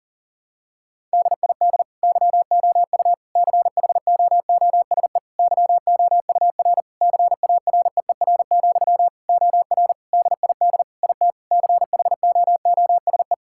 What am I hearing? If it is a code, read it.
Morse code, heard as a keyed tone, about 40 wpm, 700 Hz. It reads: DID YOU CHOOSE YOUR CAREER, OR DID IT CHOOSE
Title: Sentences from Top 1000 Words - Review Part 6 - 40wpm